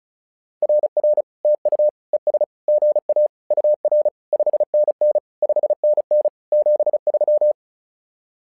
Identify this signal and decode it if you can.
Morse code, heard as a keyed tone, about 36 wpm, 600 Hz. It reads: RR TU ES GA UR 5NN 5NN 73
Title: SOTA QSO Practice 35wpm